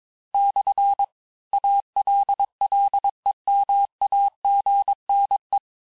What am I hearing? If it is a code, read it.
Morse code, heard as a keyed tone, about 22 wpm, 800 Hz. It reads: / ALLEMAGNE
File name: cours-du-lundi files / cours-1587-texte-clair-vitesse-16mpm-puis-21mpm